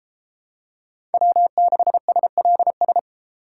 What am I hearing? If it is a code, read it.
Morse code, heard as a keyed tone, about 31 wpm, 700 Hz. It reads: W6SLS